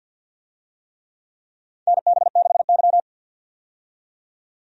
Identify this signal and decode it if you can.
Morse code, heard as a keyed tone, about 52 wpm, 700 Hz. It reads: NB6Y
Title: USA Call Signs - 4 Characters - 50wpm